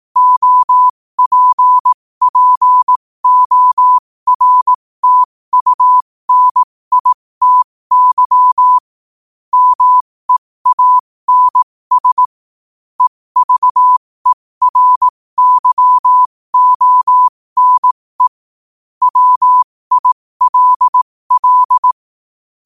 Morse code, heard as a keyed tone, about 18 wpm, spaced as for 15 wpm, 1000 Hz. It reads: OPPORTUNITY MEANS EVERYONE WILL
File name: qotdfiles / 15wpm